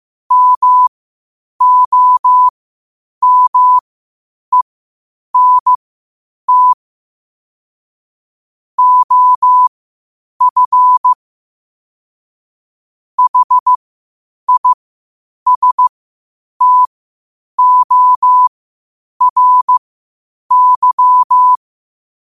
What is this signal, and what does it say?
Morse code, heard as a keyed tone, about 15 wpm, spaced as for 8 wpm, 1000 Hz. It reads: MOMENT OF HISTORY
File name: qotdfiles / 8wpm